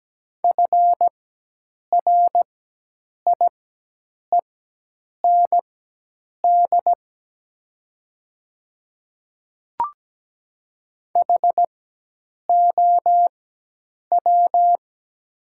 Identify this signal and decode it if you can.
Morse code, heard as a keyed tone, about 17 wpm, spaced as for 8 wpm, 700 Hz. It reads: FRIEND HOW
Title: 4x Character Spacing - Top 300 Words - 17wpm